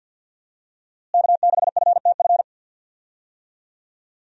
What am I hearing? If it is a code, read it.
Morse code, heard as a keyed tone, about 53 wpm, 700 Hz. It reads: K6FTF
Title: USA Call Signs - 5 Characters - Repeated 3x - 50wpm